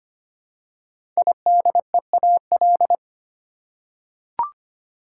Morse code, heard as a keyed tone, about 25 wpm, 700 Hz. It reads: IDEAL
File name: 5 Letter Words - First 3 Letters Given - 25wpm